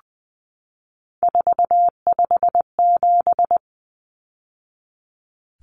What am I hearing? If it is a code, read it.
Morse code, heard as a keyed tone, about 20 wpm, 700 Hz. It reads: 457
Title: Numbers - 3-Digit 20wpm